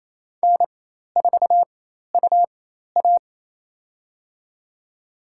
Morse code, heard as a keyed tone, about 27 wpm, spaced as for 13 wpm, 700 Hz. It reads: N4UA
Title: USA Call Signs - 4 Characters - 4x Character Spacing - 28wpm